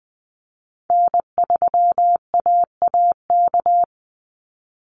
Morse code, heard as a keyed tone, about 20 wpm, 700 Hz. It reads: N3AAK
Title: Call Signs Repeated 3x - 20wpm